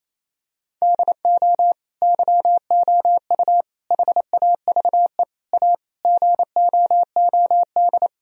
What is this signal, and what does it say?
Morse code, heard as a keyed tone, about 27 wpm, 700 Hz. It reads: DO YOU HAVE A GOOD
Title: Sentences from Top 1000 Words - Review Part 3 - 28wpm